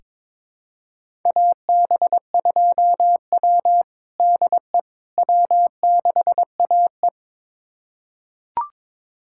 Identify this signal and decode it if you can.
Morse code, heard as a keyed tone, about 22 wpm, 700 Hz. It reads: AB2W DE W6AE
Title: USA Callsign DE Callsign - 4 Characters - 22wpm